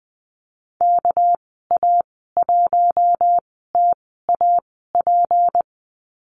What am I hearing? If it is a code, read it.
Morse code, heard as a keyed tone, about 20 wpm, spaced as for 15 wpm, 700 Hz. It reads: KA1TAP
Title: 207 Lesson 33g - Letter j - Call Signs 20wpm